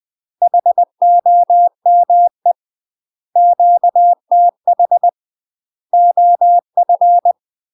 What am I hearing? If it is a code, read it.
Morse code, heard as a keyed tone, about 20 wpm, 700 Hz. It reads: HOME QTH OF
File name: UK MASTER QSOs TRAINING / 20WPM / UK MASTER QSOs 0049